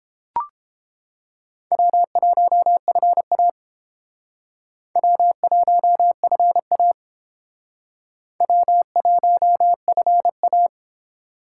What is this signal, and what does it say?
Morse code, heard as a keyed tone, about 31 wpm, 700 Hz. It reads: W1FA W1FA W1FA